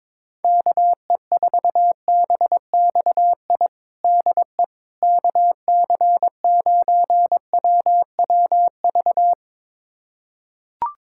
Morse code, heard as a keyed tone, about 22 wpm, 700 Hz. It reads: KE4BXI DE KC9WWV
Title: USA Callsign DE Callsign 22wpm